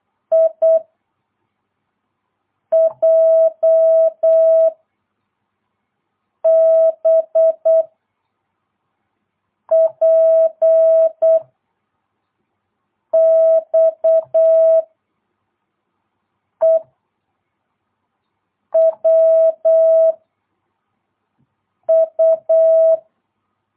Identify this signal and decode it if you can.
Morse code, heard as a keyed tone, about 8 wpm, 660 Hz. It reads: I J B P X E W U
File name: jh8cbh / 2022cw